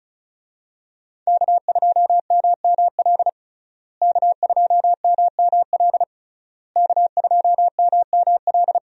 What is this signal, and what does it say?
Morse code, heard as a keyed tone, about 36 wpm, 700 Hz. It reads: K2MML K2MML K2MML
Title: USA Call Signs - 5 Characters - Repeated 3x - 35wpm